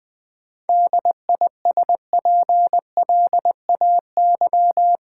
Morse code, heard as a keyed tone, about 20 wpm, 700 Hz. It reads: DISPLAY
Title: Sentences from Top 1200 Words - New Word Drill - 20wpm